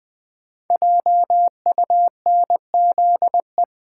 Morse code, heard as a keyed tone, about 20 wpm, 700 Hz. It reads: JUNZE
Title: Code Groups - 5 Letters - 20wpm